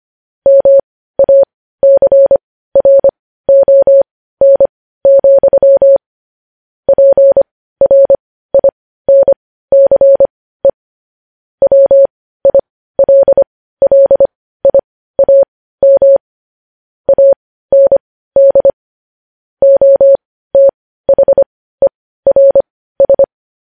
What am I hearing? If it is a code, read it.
Morse code, heard as a keyed tone, about 25 wpm, spaced as for 15 wpm, 550 Hz. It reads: MACRON, PRINCE WILLIAM AND OTHERS
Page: news-headlines-in-morse-code-at-15-wpm